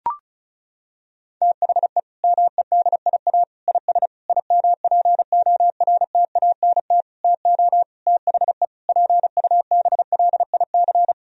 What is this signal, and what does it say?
Morse code, heard as a keyed tone, about 36 wpm, 700 Hz. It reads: THE MEDIA IS IMPORTANT TO THE PUBLIC
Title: Sentences from Top 1000 Words - Review Part 4 - 35wpm